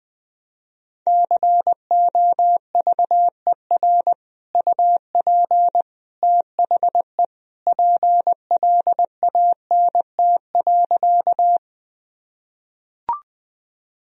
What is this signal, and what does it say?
Morse code, heard as a keyed tone, about 20 wpm, 700 Hz. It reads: COVER UP THE PLANT.
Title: Sentences from Top 1000 Words - Review Part 1 - 20wpm